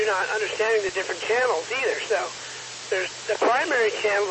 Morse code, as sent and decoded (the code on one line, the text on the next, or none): none